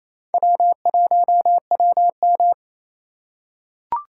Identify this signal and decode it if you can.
Morse code, heard as a keyed tone, about 28 wpm, 700 Hz. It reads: W1WM
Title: USA Call Signs - 8x Character Spacing - 28wpm